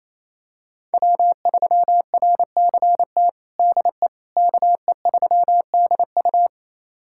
Morse code, heard as a keyed tone, about 28 wpm, 700 Hz. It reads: W3RCT DE KE3DU
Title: USA Callsign DE Callsign - 5 Characters - 28wpm